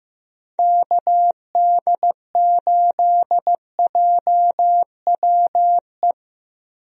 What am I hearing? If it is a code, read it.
Morse code, heard as a keyed tone, about 15 wpm, 700 Hz. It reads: KD8JWE
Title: USA Call Signs - 6 Characters - 15wpm